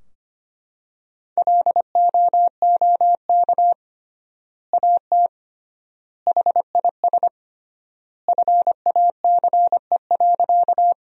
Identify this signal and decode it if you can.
Morse code, heard as a keyed tone, about 25 wpm, 700 Hz. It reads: LOOK AT HIS FACE.
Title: 3x Word-Spacing - Sentences from Top 300 Words - 25wpm